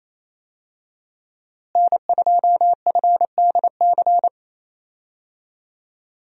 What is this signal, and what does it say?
Morse code, heard as a keyed tone, about 28 wpm, 700 Hz. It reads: N2FDC
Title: USA Call Signs - 5 Characters - Repeated 2x - 28wpm